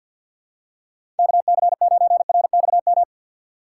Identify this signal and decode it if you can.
Morse code, heard as a keyed tone, about 48 wpm, 700 Hz. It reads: KC9RXM
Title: USA Call Signs - 6 Characters - 4x Character Spacing - 50wpm